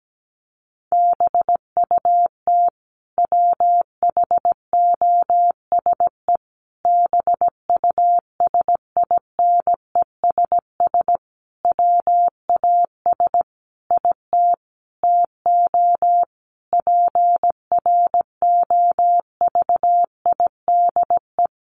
Morse code, heard as a keyed tone, about 17 wpm, 700 Hz. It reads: BUT WHOSE BUSINESS WAS IT TO PROVIDE
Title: Sentences from Top 1500 Words 17wpm